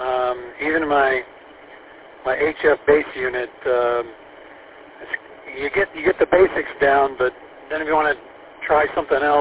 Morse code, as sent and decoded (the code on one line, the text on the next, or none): none